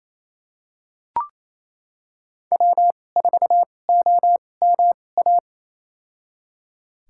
Morse code, heard as a keyed tone, about 28 wpm, spaced as for 20 wpm, 700 Hz. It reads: W4OMA
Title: USA Call Signs - 2x Character Spacing - 28wpm